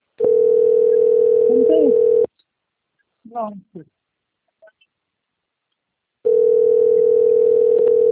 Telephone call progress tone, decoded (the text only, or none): ringback tone